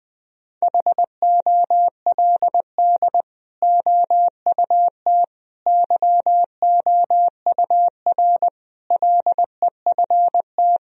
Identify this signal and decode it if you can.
Morse code, heard as a keyed tone, about 20 wpm, 700 Hz. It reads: HOLD OUT YOUR LEFT
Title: Sentences from Top 1000 Words - Review Part 2 - 20wpm